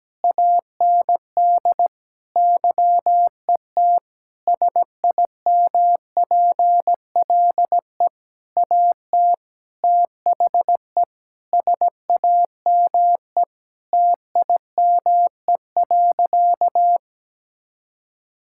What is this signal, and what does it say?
Morse code, heard as a keyed tone, about 17 wpm, 700 Hz. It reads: AND YET SIMPLE AT THE SAME TIME.
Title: Sentences from Top 1000 Words - Review Part 6 - 17wpm